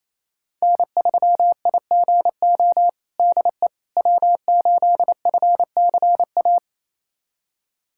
Morse code, heard as a keyed tone, about 27 wpm, 700 Hz. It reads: N3IGO DE W8FCA
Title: USA Callsign DE Callsign - 5 Characters - 28wpm